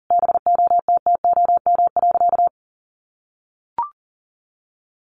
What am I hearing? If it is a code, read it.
Morse code, heard as a keyed tone, about 37 wpm, 700 Hz. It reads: BOTTOM.
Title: Easy Phrases 40wpm